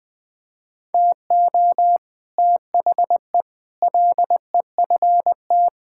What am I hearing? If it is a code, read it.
Morse code, heard as a keyed tone, about 21 wpm, 700 Hz. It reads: TO THE LEFT
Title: Sets of 9 Words - 20wpm